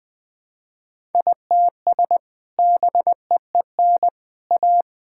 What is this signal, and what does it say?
Morse code, heard as a keyed tone, about 20 wpm, 700 Hz. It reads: ITS BEEN A
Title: Sentences with 2-Word Phrases - 20wpm